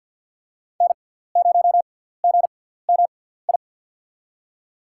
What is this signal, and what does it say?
Morse code, heard as a keyed tone, about 53 wpm, spaced as for 17 wpm, 700 Hz. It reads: N0GMI
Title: USA Call Signs - 5 Characters - 6x Character Spacing - 50wpm